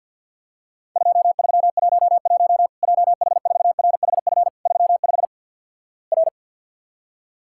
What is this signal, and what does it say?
Morse code, heard as a keyed tone, about 50 wpm, 700 Hz. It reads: 2311 JH4RHF 35 R
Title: Worked All Europe DX Contest - QTC Exchange - 50wpm